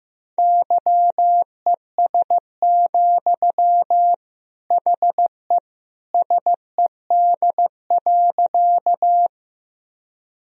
Morse code, heard as a keyed tone, about 15 wpm, 700 Hz. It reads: YES, HE SED.
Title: Sentences from Top 500 Words and Abbreviations 15wpm